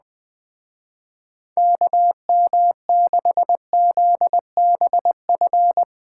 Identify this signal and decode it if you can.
Morse code, heard as a keyed tone, about 20 wpm, 700 Hz. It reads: KM6ZBF
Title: Speed Racing - USA 2x3 Call Signs 17wpm